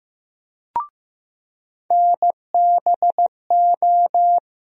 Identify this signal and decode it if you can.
Morse code, heard as a keyed tone, about 15 wpm, 700 Hz. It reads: NBO
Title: USA Call Sign Postfixes - 3 Letters 15wpm